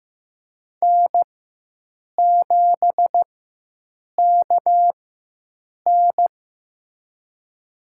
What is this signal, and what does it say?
Morse code, heard as a keyed tone, about 15 wpm, spaced as for 7 wpm, 700 Hz. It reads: N7KN
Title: USA Call Signs - 4 Characters - 4x Character Spacing - 15wpm